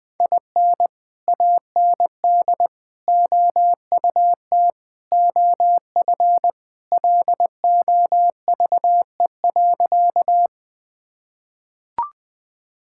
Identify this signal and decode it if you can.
Morse code, heard as a keyed tone, about 20 wpm, 700 Hz. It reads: IN AND OUT OF LOVE.
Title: Sentences with 4-Word Phrases - Spoken Beginning - 20wpm